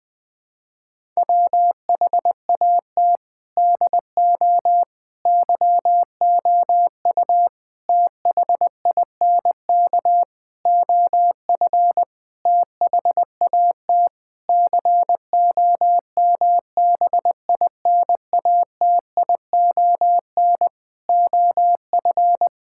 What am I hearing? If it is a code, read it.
Morse code, heard as a keyed tone, about 20 wpm, 700 Hz. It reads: WHAT DO YOU THINK OF THAT COMBINATION OF
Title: Sentences from Top 1000 Words - Review Part 6 - 20wpm